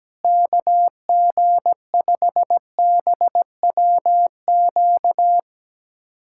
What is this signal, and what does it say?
Morse code, heard as a keyed tone, about 17 wpm, 700 Hz. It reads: KG5BWQ
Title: USA Callsign DE Callsign 17wpm